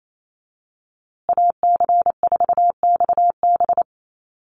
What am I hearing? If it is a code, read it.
Morse code, heard as a keyed tone, about 28 wpm, 700 Hz. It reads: AC4XB
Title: USA Call Signs Repeated 2x 28wpm